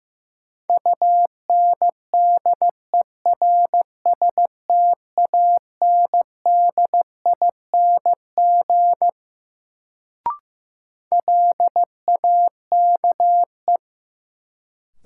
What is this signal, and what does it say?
Morse code, heard as a keyed tone, about 15 wpm, 700 Hz. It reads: UNDERSTANDING LAKE
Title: Sentences from Top 1000 Words - Review Part 3 - New Word Drill - 15wpm